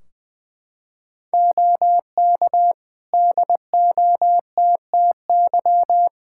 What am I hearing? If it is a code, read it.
Morse code, heard as a keyed tone, about 21 wpm, 700 Hz. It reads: OK DOTTY
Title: Practice QSO - Exchange 4 - Ending - 20wpm